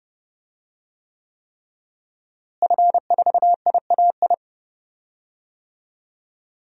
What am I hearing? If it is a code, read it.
Morse code, heard as a keyed tone, about 29 wpm, 700 Hz. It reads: F4IAI
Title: DXCC-100 Call Signs - 5 Characters - Repeated 3x - 30wpm